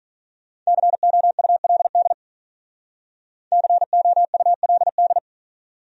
Morse code, heard as a keyed tone, about 45 wpm, 700 Hz. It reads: COULD COULD